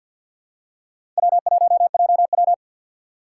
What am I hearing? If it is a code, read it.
Morse code, heard as a keyed tone, about 50 wpm, 700 Hz. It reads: W1JW
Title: USA Call Signs - 4 Characters - 3x Character Spacing - 50wpm